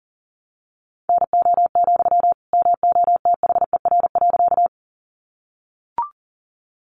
Morse code, heard as a keyed tone, about 40 wpm, 700 Hz. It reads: NO, MOTHER.
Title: Sentences from Top 200 Words 40wpm